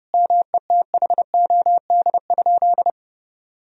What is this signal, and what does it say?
Morse code, heard as a keyed tone, about 29 wpm, 700 Hz. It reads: METHOD?
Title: Sentences from Top 1000 Words - Review Part 3 - 30wpm